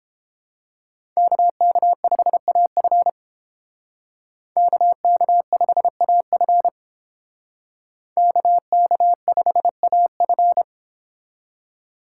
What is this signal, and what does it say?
Morse code, heard as a keyed tone, about 30 wpm, 700 Hz. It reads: KK5AF KK5AF KK5AF